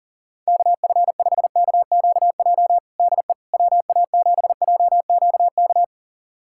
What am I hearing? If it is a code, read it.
Morse code, heard as a keyed tone, about 40 wpm, 700 Hz. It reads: KF5KQJ DE WA7JQK
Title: USA Callsign DE Callsign 40wpm